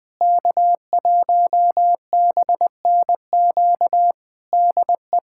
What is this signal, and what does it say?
Morse code, heard as a keyed tone, about 20 wpm, 700 Hz. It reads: K1BNQ DE
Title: Practice QSO - Exchange 4 - Ending - 20wpm